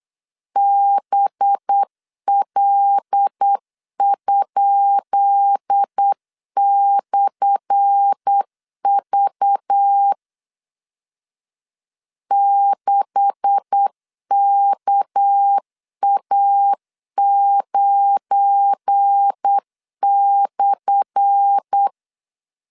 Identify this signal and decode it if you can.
Morse code, heard as a keyed tone, about 8 wpm, 800 Hz. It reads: BL?/V 6KA9/
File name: LEZIONI / 10 SIGNS / 10B INT 8wpm 3534